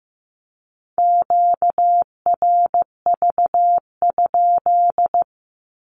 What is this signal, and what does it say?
Morse code, heard as a keyed tone, about 15 wpm, 700 Hz. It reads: QRV?
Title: Q-Codes with Spoken Meaning - 15wpm